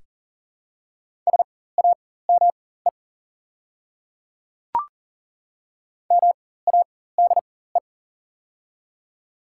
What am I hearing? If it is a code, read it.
Morse code, heard as a keyed tone, about 40 wpm, spaced as for 19 wpm, 700 Hz. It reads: SAME MADE
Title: 4x Character Spacing - Top 200 Words - 40wpm